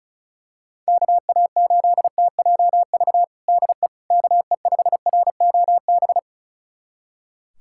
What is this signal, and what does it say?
Morse code, heard as a keyed tone, about 36 wpm, 700 Hz. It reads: KA8TJV DE KE5ROB
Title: USA Callsign DE Callsign 35wpm